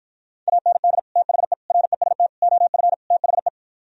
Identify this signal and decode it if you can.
Morse code, heard as a keyed tone, about 52 wpm, 700 Hz. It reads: AND THE REST OF THE